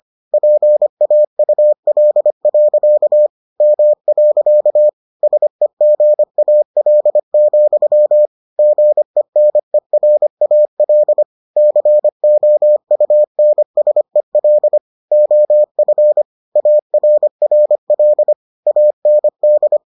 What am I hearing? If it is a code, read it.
Morse code, heard as a keyed tone, about 25 wpm, 600 Hz. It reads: PAUL. M. SEGAL, GENERAL COUNSEL OF ARRL AND